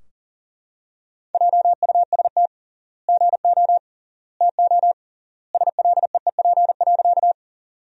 Morse code, heard as a keyed tone, about 40 wpm, 700 Hz. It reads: JUST GO TO SLEEP.
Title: 3x Word-Spacing - Sentences from Top 500 Words - 40wpm